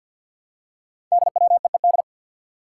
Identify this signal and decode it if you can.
Morse code, heard as a keyed tone, about 47 wpm, 700 Hz. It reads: DWEED
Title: Code Groups - 5 Letter:Numbers - 50wpm